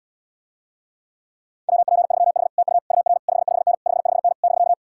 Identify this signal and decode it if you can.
Morse code, heard as a keyed tone, about 80 wpm, 700 Hz. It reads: WP3H TU UR 55N 55N <BK>